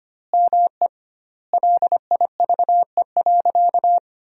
Morse code, heard as a keyed tone, about 25 wpm, 700 Hz. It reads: ME LIVE.
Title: 2x Word-Spacing - Sentences from Top 300 Words - 25wpm